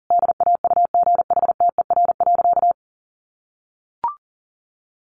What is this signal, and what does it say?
Morse code, heard as a keyed tone, about 37 wpm, 700 Hz. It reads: DAUGHTER.
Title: Sentences from Top 1500 Words 40wpm